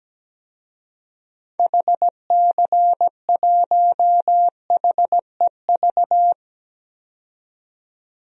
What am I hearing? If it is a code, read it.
Morse code, heard as a keyed tone, about 17 wpm, 700 Hz. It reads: HC1HEV
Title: DXCC-100 Call Signs - 6 Characters - 17wpm